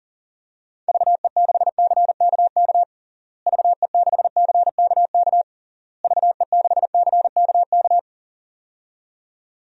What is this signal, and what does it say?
Morse code, heard as a keyed tone, about 40 wpm, 700 Hz. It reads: VE6CKK VE6CKK VE6CKK
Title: DXCC-100 Call Signs - 6 Characters - Repeated 3x - 40wpm